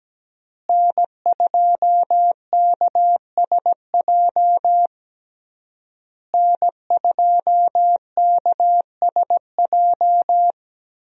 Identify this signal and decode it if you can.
Morse code, heard as a keyed tone, about 17 wpm, 700 Hz. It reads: N2KSJ N2KSJ
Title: USA Call Signs - 5 Characters - Repeated 3x - 17wpm